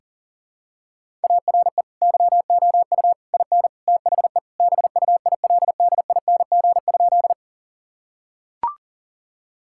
Morse code, heard as a keyed tone, about 40 wpm, 700 Hz. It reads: ARE YOU IN THE BUILDING?
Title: Sentences from Top 1000 Words - Review Part 5 - 40wpm